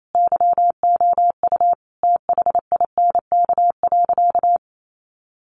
Morse code, heard as a keyed tone, about 28 wpm, 700 Hz. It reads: YOU THINK.
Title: Sentences from Top 400 Words 28wpm